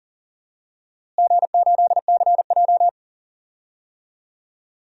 Morse code, heard as a keyed tone, about 40 wpm, 700 Hz. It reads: G8CJ
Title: DXCC-100 Call Signs - 4 Characters - Repeated 3x - 40wpm